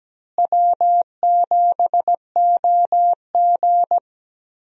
Morse code, heard as a keyed tone, about 17 wpm, 700 Hz. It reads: W7OG
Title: USA Call Signs - 4 Characters - Repeated 3x - 17wpm